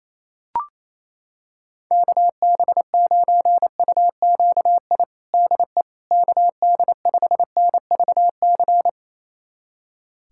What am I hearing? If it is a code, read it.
Morse code, heard as a keyed tone, about 28 wpm, 700 Hz. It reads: KB9UQI DE KD5NVC
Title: USA Callsign DE Callsign - 6 Characters - 28wpm